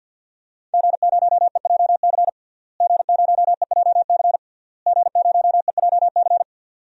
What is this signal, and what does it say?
Morse code, heard as a keyed tone, about 52 wpm, 700 Hz. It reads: G0EJC G0EJC G0EJC
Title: DXCC-100 Call Signs - 5 Characters - Repeated 3x - 50wpm